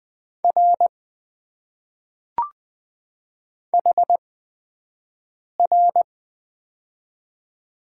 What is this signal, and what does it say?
Morse code, heard as a keyed tone, about 20 wpm, spaced as for 5 wpm, 700 Hz. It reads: R HR
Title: Top 90 QSO Elements - 8x Character Spacing - 20wpm